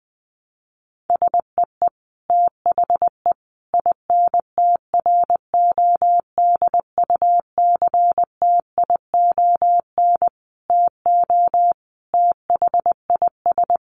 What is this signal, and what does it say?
Morse code, heard as a keyed tone, about 20 wpm, 700 Hz. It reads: SEE THE INTRODUCTION TO THIS
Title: Sentences from Top 1300 Words 20wpm